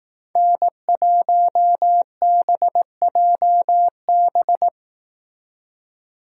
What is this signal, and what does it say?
Morse code, heard as a keyed tone, about 18 wpm, 700 Hz. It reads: N1BJB
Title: Speed Racing - USA 1x3 Call Signs 15wpm